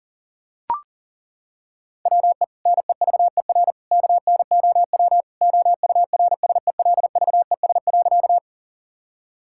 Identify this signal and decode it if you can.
Morse code, heard as a keyed tone, about 40 wpm, 700 Hz. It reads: WE NEVER KNOW OURSELVES.
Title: Sentences from Top 1600 Words 40wpm